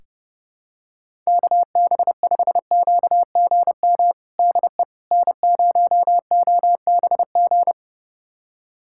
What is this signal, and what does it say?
Morse code, heard as a keyed tone, about 30 wpm, 700 Hz. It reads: KB5QGM DE N0OBG
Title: USA Callsign DE Callsign 30wpm